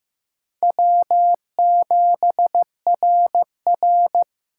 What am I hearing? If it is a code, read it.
Morse code, heard as a keyed tone, about 15 wpm, 700 Hz. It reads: W7RR
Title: USA Call Signs - 4 Characters - 4x Character Spacing - 15wpm